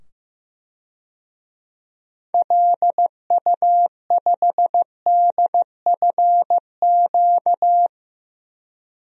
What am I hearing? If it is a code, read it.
Morse code, heard as a keyed tone, about 15 wpm, 700 Hz. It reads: LU5DFQ
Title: DXCC-100 Call Signs - 6 Characters - 15wpm